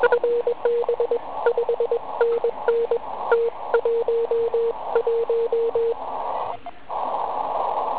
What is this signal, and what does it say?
Morse code, heard as a keyed tone, about 21 wpm, 470 Hz, with radio noise. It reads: FB 5NN T11
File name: Archiv / CUC138 / OK1IF CUC138